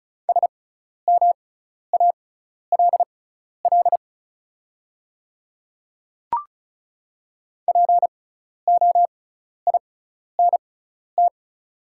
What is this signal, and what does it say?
Morse code, heard as a keyed tone, about 36 wpm, spaced as for 12 wpm, 700 Hz. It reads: SMALL POINT
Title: 6x Character Spacing - Top 200 Words - 35wpm